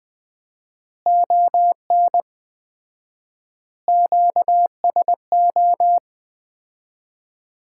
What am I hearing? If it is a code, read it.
Morse code, heard as a keyed tone, about 21 wpm, 700 Hz. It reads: ON QSO
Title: Common QSO Phrases - 4x Word-Spacing - 20wpm